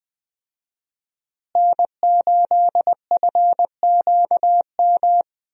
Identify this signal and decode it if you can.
Morse code, heard as a keyed tone, about 21 wpm, 700 Hz. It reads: N8FQM
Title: USA Call Signs - 5 Characters - Repeated 2x - 20wpm